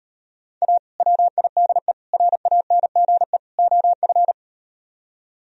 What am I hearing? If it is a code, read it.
Morse code, heard as a keyed tone, about 39 wpm, 700 Hz. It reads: A WIDE RANGE OF